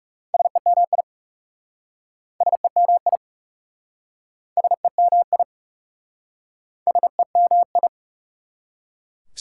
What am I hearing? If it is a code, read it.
Morse code, heard as a keyed tone, about 38 wpm, 690 Hz. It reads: SEMI SEMI SEMI SEMI